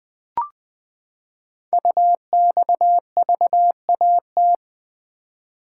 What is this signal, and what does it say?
Morse code, heard as a keyed tone, about 20 wpm, 700 Hz. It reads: UXVAT
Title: Code Groups - 5 Letters - 20wpm